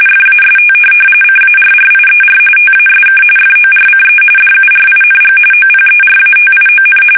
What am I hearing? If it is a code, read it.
RTTY, about 45.45 baud. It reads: FAST CARRIER TASK FORCE DURING OPERATION
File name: fsk850 / USS-Massachusetts